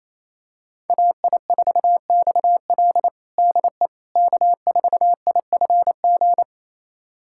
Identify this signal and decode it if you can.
Morse code, heard as a keyed tone, about 28 wpm, 700 Hz. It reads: AI4XL DE K4IFG
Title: USA Callsign DE Callsign - 5 Characters - 28wpm